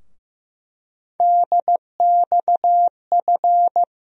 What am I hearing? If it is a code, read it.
Morse code, heard as a keyed tone, about 15 wpm, 700 Hz. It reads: DXF
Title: USA Call Sign Postfixes - 3 Letters 15wpm